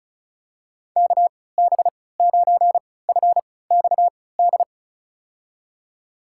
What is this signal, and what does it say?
Morse code, heard as a keyed tone, about 36 wpm, spaced as for 20 wpm, 700 Hz. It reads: KB9FXD
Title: USA Call Signs - 6 Characters - 3x Character Spacing - 35wpm